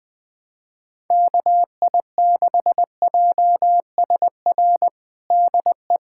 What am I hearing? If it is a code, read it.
Morse code, heard as a keyed tone, about 20 wpm, 700 Hz. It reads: KI6JSR DE
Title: USA Callsign DE Callsign - 6 Characters - 20wpm